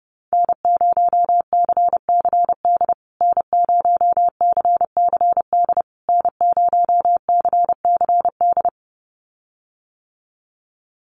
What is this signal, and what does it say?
Morse code, heard as a keyed tone, about 30 wpm, 700 Hz. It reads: N0CCD N0CCD N0CCD
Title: USA Call Signs Repeated 3x 30wpm